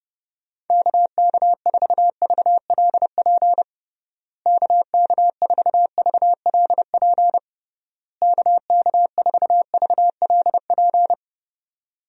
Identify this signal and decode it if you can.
Morse code, heard as a keyed tone, about 32 wpm, 700 Hz. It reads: KK4VLP KK4VLP KK4VLP
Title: USA Call Signs - 6 Characters - Repeated 3x - 30wpm